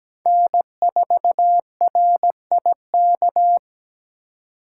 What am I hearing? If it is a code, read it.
Morse code, heard as a keyed tone, about 17 wpm, 700 Hz. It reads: N4RIK
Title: USA Call Signs - 5 Characters - 3x Character Spacing - 17wpm